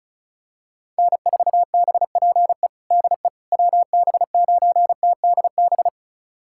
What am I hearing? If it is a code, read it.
Morse code, heard as a keyed tone, about 36 wpm, 700 Hz. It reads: N4BPE DE WB9TDB
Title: USA Callsign DE Callsign 35wpm